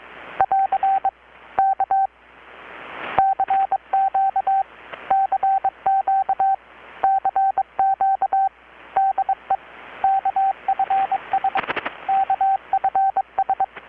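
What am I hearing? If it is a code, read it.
Morse code, heard as a keyed tone, about 23 wpm, 780 Hz, with radio noise. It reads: <AR> K CQ CQ CQ DE KFS KFS
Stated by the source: Morse from KPH's Night of Nights XX maritime radiotelegraphy event, received in Chiapas, Mexico about 3,800 km away on a Kenwood R-600 with a grounded 10 m long-wire antenna.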